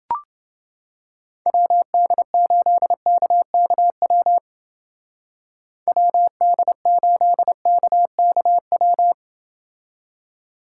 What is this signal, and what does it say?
Morse code, heard as a keyed tone, about 29 wpm, 700 Hz. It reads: WD8KKW WD8KKW